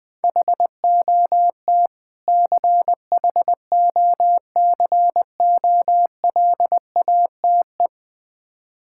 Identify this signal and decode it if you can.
Morse code, heard as a keyed tone, about 20 wpm, 700 Hz. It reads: HOT CHOCOLATE
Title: Difficult Words with Hints - 20wpm